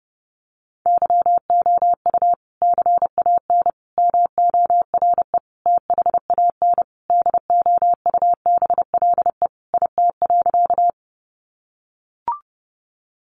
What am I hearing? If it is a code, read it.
Morse code, heard as a keyed tone, about 28 wpm, 700 Hz. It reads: YOU CAN MORE THAN DOUBLE IT.
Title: Sentences from Top 1400 Words 30wpm